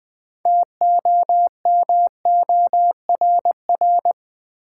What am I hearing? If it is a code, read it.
Morse code, heard as a keyed tone, about 20 wpm, 700 Hz. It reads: TOMORR
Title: Build A Word - 3 Letters At A Time - 8-Letter Words - 20wpm